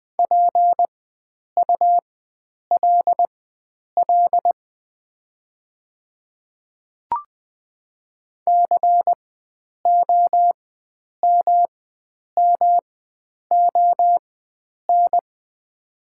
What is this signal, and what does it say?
Morse code, heard as a keyed tone, about 20 wpm, spaced as for 9 wpm, 700 Hz. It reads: PULL COMMON
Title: 4x Character Spacing - Top 500 Words - 20wpm